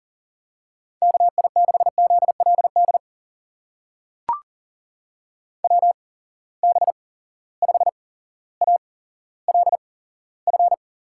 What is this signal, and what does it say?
Morse code, heard as a keyed tone, about 40 wpm, spaced as for 17 wpm, 700 Hz. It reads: KI6ZLD WB5ALF
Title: USA Call Signs - 6 Characters - 8x Character Spacing - 40wpm